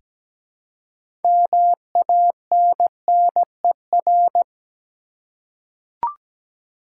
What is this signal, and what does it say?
Morse code, heard as a keyed tone, about 17 wpm, 700 Hz. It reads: MANNER
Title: Six Letter Words - 17wpm